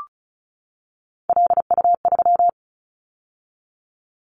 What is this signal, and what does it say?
Morse code, heard as a keyed tone, about 36 wpm, 700 Hz. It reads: LU3
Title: Callsign Prefixes for Top 100 DXCC - 35wpm